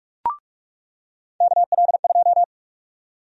Morse code, heard as a keyed tone, about 46 wpm, 700 Hz. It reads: KL2
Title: USA Call Sign Prefixes - 45wpm